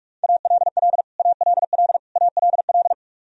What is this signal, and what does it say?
Morse code, heard as a keyed tone, about 46 wpm, 700 Hz. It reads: ALL ALL ALL
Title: Top 100 Words Repeated 45wpm